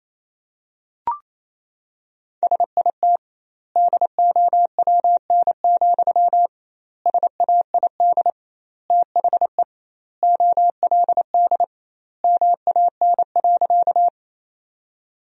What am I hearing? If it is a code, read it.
Morse code, heard as a keyed tone, about 28 wpm, 700 Hz. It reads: SIT DOWN, SAID THE OLD MAN.
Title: 2x Word-Spacing - Sentences from Top 500 Words - 28wpm